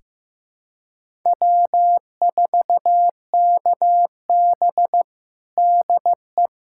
Morse code, heard as a keyed tone, about 15 wpm, 700 Hz. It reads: W4KB DE
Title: USA Callsign DE Callsign - 4 Characters - 15wpm